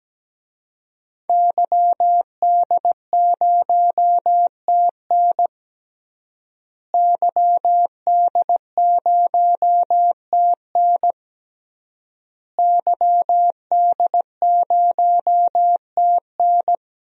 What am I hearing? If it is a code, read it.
Morse code, heard as a keyed tone, about 17 wpm, 700 Hz. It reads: YD0TN YD0TN YD0TN
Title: DXCC-100 Call Signs - 5 Characters - Repeated 3x - 17wpm